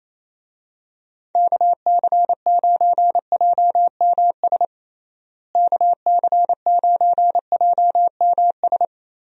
Morse code, heard as a keyed tone, about 27 wpm, 700 Hz. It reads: KC9JMS KC9JMS
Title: USA Call Signs - 6 Characters - Repeated 2x - 28wpm